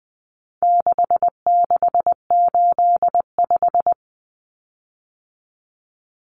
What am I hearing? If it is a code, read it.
Morse code, heard as a keyed tone, about 20 wpm, 700 Hz. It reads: 6685
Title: Common Cut Numbers - 4-Digit 20wpm